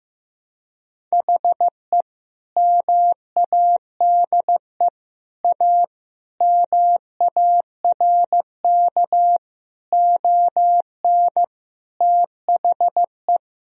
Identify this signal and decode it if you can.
Morse code, heard as a keyed tone, about 15 wpm, 700 Hz. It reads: HE MADE A MARK ON THE
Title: Sentences from Top 1000 Words - Review Part 1 - 15wpm